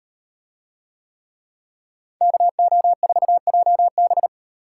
Morse code, heard as a keyed tone, about 38 wpm, 700 Hz. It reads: KO4JB